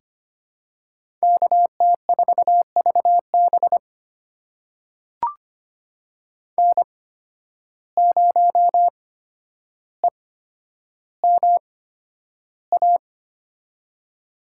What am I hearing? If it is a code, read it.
Morse code, heard as a keyed tone, about 25 wpm, spaced as for 11 wpm, 700 Hz. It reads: KT4VB N0EMA
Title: USA Call Signs - 5 Characters - 8x Character Spacing - 25wpm